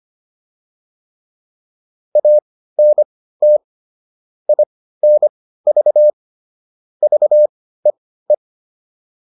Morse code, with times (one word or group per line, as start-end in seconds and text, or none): 2.2-3.6 ANT
4.5-6.1 INV
7.0-8.3 VEE